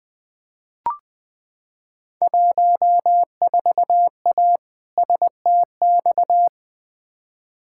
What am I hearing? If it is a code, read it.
Morse code, heard as a keyed tone, about 20 wpm, 700 Hz. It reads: 14A STX
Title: Field Day Exchanges Encore - 20wpm